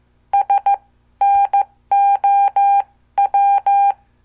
Morse code, heard as a keyed tone, about 14 wpm, spaced as for 12 wpm, 800 Hz. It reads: SNOW
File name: morse / qso